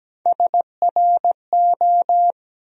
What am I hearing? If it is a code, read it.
Morse code, heard as a keyed tone, about 17 wpm, 700 Hz. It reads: SRO
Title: USA Call Sign Postfixes - 3 Letters 17wpm